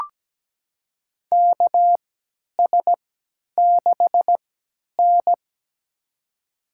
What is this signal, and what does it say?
Morse code, heard as a keyed tone, about 17 wpm, spaced as for 10 wpm, 700 Hz. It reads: KS6N
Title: USA Call Signs - 4 Characters - 3x Character Spacing - 17wpm